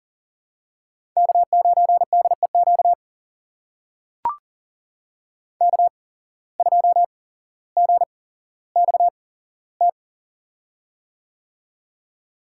Morse code, heard as a keyed tone, about 40 wpm, spaced as for 17 wpm, 700 Hz. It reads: K9DEQ K2GXT
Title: USA Call Signs - 5 Characters - 8x Character Spacing - 40wpm